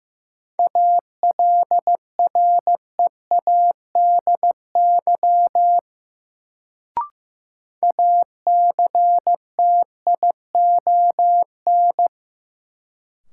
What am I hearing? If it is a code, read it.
Morse code, heard as a keyed tone, about 15 wpm, 700 Hz. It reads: ALREADY ACTION
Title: Sentences from Top 700 Words - New Word Drill - 15wpm